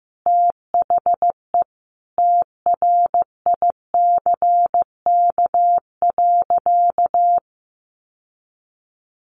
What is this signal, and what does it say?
Morse code, heard as a keyed tone, about 15 wpm, 700 Hz. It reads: THE TRICK.
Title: Sentences from Top 1500 Words 15wpm